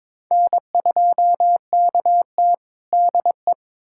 Morse code, heard as a keyed tone, about 22 wpm, 700 Hz. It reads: N2KT DE
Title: USA Callsign DE Callsign - 4 Characters - 22wpm